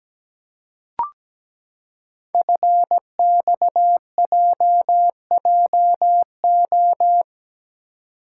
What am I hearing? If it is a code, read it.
Morse code, heard as a keyed tone, about 17 wpm, 700 Hz. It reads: FXJJO
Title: Code Groups - 5 Letters - 17wpm